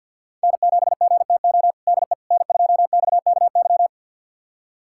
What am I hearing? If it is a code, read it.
Morse code, heard as a keyed tone, about 51 wpm, 700 Hz. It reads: N7GTO DE N2XKY
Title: USA Callsign DE Callsign - 5 Characters - 50wpm